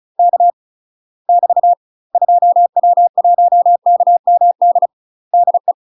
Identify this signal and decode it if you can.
Morse code, heard as a keyed tone, about 36 wpm, 700 Hz. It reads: K <BT> 2W1KMD DE
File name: UK MASTER QSOs TRAINING / 35WPM / UK MASTER QSOs 0033